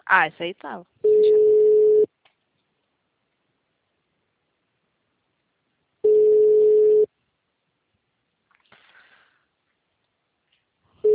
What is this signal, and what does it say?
Ringback tone (Europe, 425 Hz), with speech.